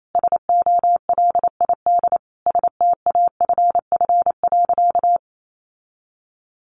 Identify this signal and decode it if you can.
Morse code, heard as a keyed tone, about 28 wpm, 700 Hz. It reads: SOLID STAFF.
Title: Sentences from Top 1500 Words 28wpm